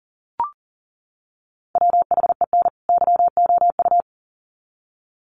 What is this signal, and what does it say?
Morse code, heard as a keyed tone, about 40 wpm, 700 Hz. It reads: WHEN YOU
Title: Sets of 2 Words 40wpm